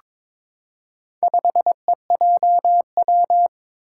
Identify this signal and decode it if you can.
Morse code, heard as a keyed tone, about 23 wpm, 700 Hz. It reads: 5EJW
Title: Four Challenging Letter-Number Combinations 22wpm